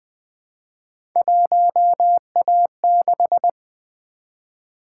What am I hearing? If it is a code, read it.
Morse code, heard as a keyed tone, about 20 wpm, 700 Hz. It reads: 1A6
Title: Three Letter-Number Combinations - 20wpm